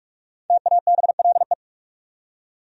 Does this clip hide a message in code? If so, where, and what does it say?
Yes: Morse code from 0.5 to 1.5 s, heard as a keyed tone, about 59 wpm, 700 Hz. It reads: TABLE